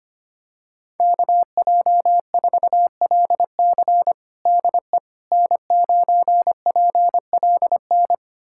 Morse code, heard as a keyed tone, about 25 wpm, 700 Hz. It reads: KJ4LC DE N9PLN
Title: USA Callsign DE Callsign - 5 Characters - 25wpm